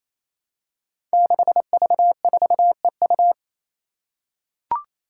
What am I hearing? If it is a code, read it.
Morse code, heard as a keyed tone, about 29 wpm, 700 Hz. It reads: 6V4EU
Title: Five Challenging Letter-Number Combinations 28wpm